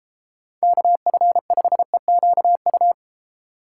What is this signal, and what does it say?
Morse code, heard as a keyed tone, about 33 wpm, 700 Hz. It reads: KF5EQU